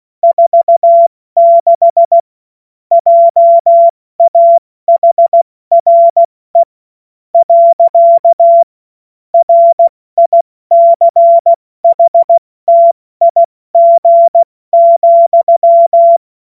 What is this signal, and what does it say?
Morse code, heard as a keyed tone, about 16 wpm, spaced as for 13 wpm, 680 Hz. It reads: 46 JAHRE. RICHTIG,
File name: Morsekurs / Witze / Witz 14. September 2021, 70 F80